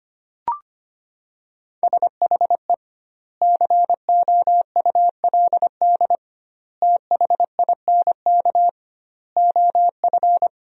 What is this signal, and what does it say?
Morse code, heard as a keyed tone, about 25 wpm, 700 Hz. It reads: SHE COULD THINK OF
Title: 2x Word-Spacing - Sentences from Top 500 Words - 25wpm